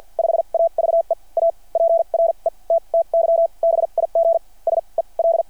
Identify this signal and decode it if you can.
Morse code, heard as a keyed tone, about 53 wpm, spaced as for 37 wpm, 660 Hz. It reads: 5AVE A PRETTY BIG SEL
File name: archived / CRSnet